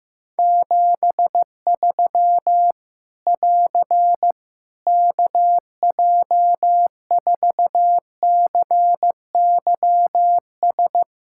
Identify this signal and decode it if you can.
Morse code, heard as a keyed tone, about 15 wpm, 700 Hz. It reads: 73 <AR> KJ4CYS
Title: Practice QSO - Exchange 4 - Ending - 15wpm